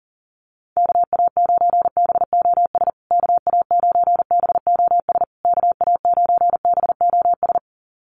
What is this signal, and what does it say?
Morse code, heard as a keyed tone, about 40 wpm, 700 Hz. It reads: KA9BOS KA9BOS KA9BOS
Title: USA Call Signs Repeated 3x 40wpm